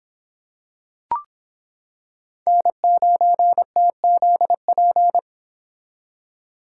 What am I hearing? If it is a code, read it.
Morse code, heard as a keyed tone, about 26 wpm, 700 Hz. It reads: N9TZP